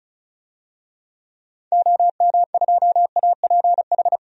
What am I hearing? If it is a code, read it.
Morse code, heard as a keyed tone, about 36 wpm, 700 Hz. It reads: OM2APH
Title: DXCC-100 Call Signs - 6 Characters - 35wpm